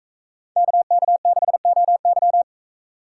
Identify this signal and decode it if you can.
Morse code, heard as a keyed tone, about 41 wpm, 700 Hz. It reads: KK6OY